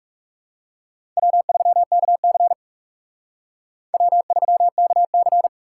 Morse code, heard as a keyed tone, about 38 wpm, 700 Hz. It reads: W3KC W3KC